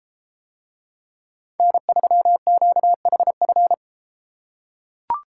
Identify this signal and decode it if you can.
Morse code, heard as a keyed tone, about 33 wpm, 700 Hz. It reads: N3QHF